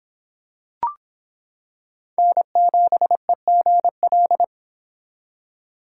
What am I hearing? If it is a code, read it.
Morse code, heard as a keyed tone, about 26 wpm, 700 Hz. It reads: N7EGL